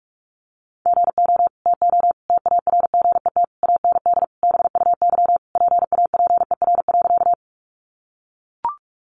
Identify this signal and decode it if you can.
Morse code, heard as a keyed tone, about 44 wpm, 700 Hz. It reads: GO TO TARGET AND BUY PAPER.
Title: Sentences from Top 1200 Words 45wpm